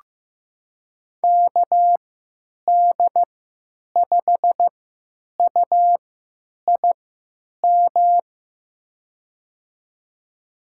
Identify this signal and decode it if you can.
Morse code, heard as a keyed tone, about 15 wpm, spaced as for 9 wpm, 700 Hz. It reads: KD5UIM
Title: USA Call Signs - 6 Characters - 3x Character Spacing - 15wpm